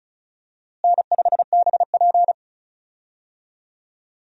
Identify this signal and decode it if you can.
Morse code, heard as a keyed tone, about 36 wpm, 700 Hz. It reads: N5BP
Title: USA Call Signs - 4 Characters - Repeated 3x - 35wpm